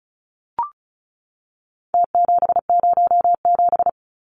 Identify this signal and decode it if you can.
Morse code, heard as a keyed tone, about 34 wpm, 700 Hz. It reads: T707
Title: Numbers - Contest Style - 4-Digit 35wpm